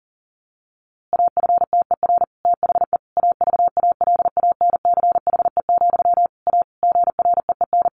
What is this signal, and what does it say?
Morse code, heard as a keyed tone, about 40 wpm, 700 Hz. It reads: AFTER THE AVALANCHE, A GREEN
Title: A Sound Of Thunder 40wpm